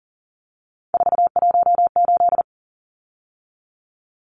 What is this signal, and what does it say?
Morse code, heard as a keyed tone, about 40 wpm, 700 Hz. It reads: 418
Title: Numbers - 3-Digit 40wpm